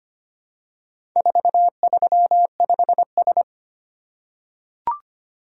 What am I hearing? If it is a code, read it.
Morse code, heard as a keyed tone, about 25 wpm, 700 Hz. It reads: 435S
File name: practice / 25wpm / Four Letter-Number Combinations - 25wpm